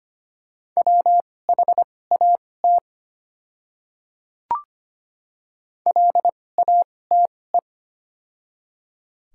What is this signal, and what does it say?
Morse code, heard as a keyed tone, about 25 wpm, spaced as for 18 wpm, 700 Hz. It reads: WHAT LATE
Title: 2x Character Spacing - Top 500 Words - 25wpm